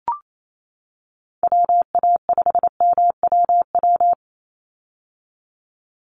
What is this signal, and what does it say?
Morse code, heard as a keyed tone, about 28 wpm, 700 Hz. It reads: WA5MWW
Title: USA Call Signs 28wpm